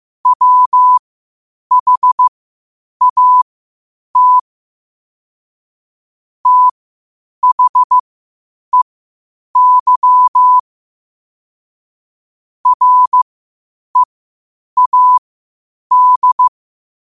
Morse code, heard as a keyed tone, about 15 wpm, spaced as for 8 wpm, 1000 Hz. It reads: WHAT THEY READ
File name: qotdfiles / 8wpm